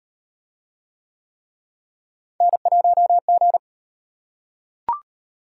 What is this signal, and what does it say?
Morse code, heard as a keyed tone, about 37 wpm, 700 Hz. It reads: N1G